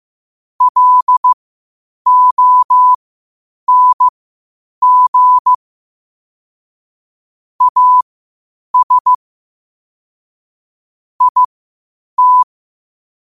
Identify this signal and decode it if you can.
Morse code, heard as a keyed tone, about 15 wpm, spaced as for 8 wpm, 1000 Hz. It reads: LONG AS IT
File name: 8wpm